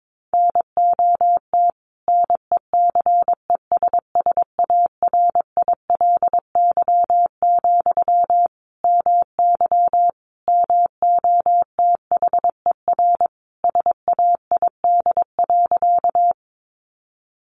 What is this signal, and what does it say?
Morse code, heard as a keyed tone, about 22 wpm, 700 Hz. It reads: NOT NECESSARILY, MY MOTHER SAID.
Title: Sentences from Top 1400 Words 22wpm